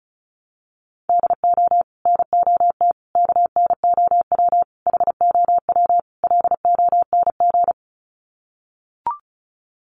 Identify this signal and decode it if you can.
Morse code, heard as a keyed tone, about 34 wpm, 700 Hz. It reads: DO NOT KNOW HOW LONG
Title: Sets of 5 Words Encore2 35wpm